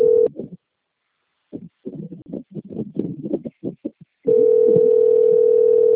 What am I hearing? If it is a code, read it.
Ringback tone (North America), 440 + 480 Hz.
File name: out- - - - - .42441